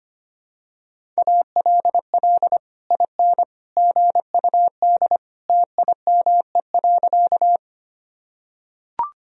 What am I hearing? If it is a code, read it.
Morse code, heard as a keyed tone, about 25 wpm, 700 Hz. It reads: ALL IN GUD TIME.
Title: Sentences from Top 200 Words and Abbreviations 25wpm